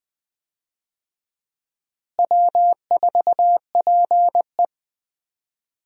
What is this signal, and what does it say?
Morse code, heard as a keyed tone, about 21 wpm, 700 Hz. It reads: W4PE
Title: USA Call Signs - 4 Characters - Repeated 3x - 20wpm